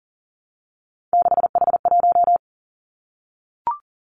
Morse code, heard as a keyed tone, about 42 wpm, 690 Hz. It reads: 6H1
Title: Three Challenging Letter-Number Combinations 40wpm